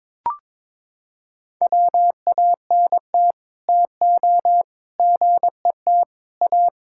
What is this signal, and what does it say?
Morse code, heard as a keyed tone, about 22 wpm, 700 Hz. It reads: WANT TO GET A
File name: practice / 22wpm / Sets of 7 Words - 22wpm